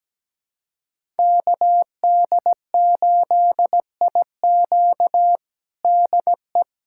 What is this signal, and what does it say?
Morse code, heard as a keyed tone, about 17 wpm, 700 Hz. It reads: KD8IQ DE
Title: USA Callsign DE Callsign - 5 Characters - 17wpm